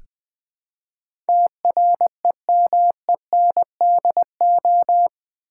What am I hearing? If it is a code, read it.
Morse code, heard as a keyed tone, about 20 wpm, 700 Hz. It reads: TREMENDO
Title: Build A Word - 4 Letters At A Time - 10-Letter Words - 20wpm